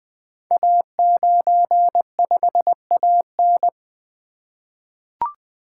Morse code, heard as a keyed tone, about 20 wpm, 700 Hz. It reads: A95AN
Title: Code Groups - 5 Letter:Numbers - 20wpm